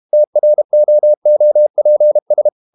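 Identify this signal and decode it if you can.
Morse code, heard as a keyed tone, about 32 wpm, 600 Hz. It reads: TROOPS
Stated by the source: Morse code practice at 32 wpm from CW med Gnister, reading Ken Follett's novel Never in English.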